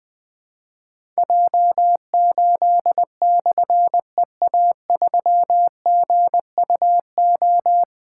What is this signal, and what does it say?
Morse code, heard as a keyed tone, about 20 wpm, 700 Hz. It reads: J8/EA3GUO
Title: Difficult Call Signs 20wpm